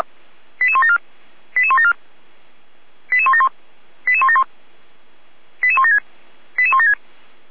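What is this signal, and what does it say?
Five-tone selective call (ZVEI), heard as quick runs of five tones, tones of about 70 ms. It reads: 88174 88174 88171 88171 88176 88176
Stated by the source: Five-tone ZVEI selective call 88169 received by SDR.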